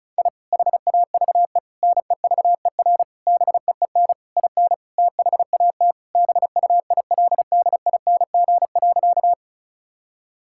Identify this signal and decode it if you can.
Morse code, heard as a keyed tone, about 34 wpm, 700 Hz. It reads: I HAVE NEVER BEEN IN THAT BUILDING.
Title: Sentences from Top 1000 Words - Review Part 5 - 35wpm